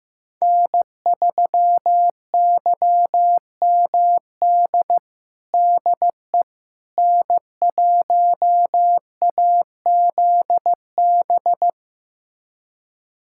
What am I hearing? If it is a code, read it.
Morse code, heard as a keyed tone, about 15 wpm, 700 Hz. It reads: N3YMD DE N1AZB
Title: USA Callsign DE Callsign - 5 Characters - 15wpm